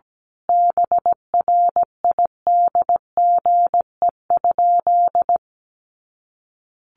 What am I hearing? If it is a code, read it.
Morse code, heard as a keyed tone, about 17 wpm, 700 Hz. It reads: BRIDGE?
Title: Sentences from Top 1200 Words 17wpm